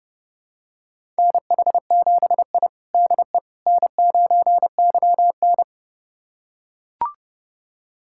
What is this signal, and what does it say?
Morse code, heard as a keyed tone, about 32 wpm, 700 Hz. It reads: NH7I DE N9YN
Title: USA Callsign DE Callsign - 4 Characters - 30wpm